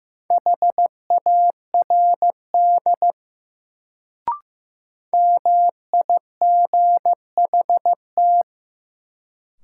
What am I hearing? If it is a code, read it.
Morse code, heard as a keyed tone, about 15 wpm, 700 Hz. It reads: HARD MIGHT
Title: Sentences from Top 300 Words - New Word Drill - 15wpm